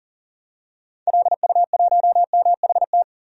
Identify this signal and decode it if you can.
Morse code, heard as a keyed tone, about 40 wpm, 700 Hz. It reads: LU1MHT
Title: DXCC-100 Call Signs - 6 Characters - 40wpm